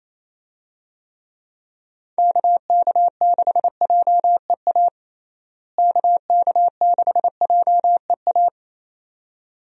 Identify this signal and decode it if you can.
Morse code, heard as a keyed tone, about 28 wpm, 700 Hz. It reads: KK6JEA KK6JEA
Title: USA Call Signs - 6 Characters - Repeated 2x - 28wpm